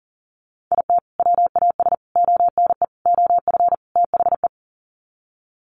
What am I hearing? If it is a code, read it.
Morse code, heard as a keyed tone, about 40 wpm, 700 Hz. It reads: IT WAS ONE OF THE
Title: Sets of 5 Words 40wpm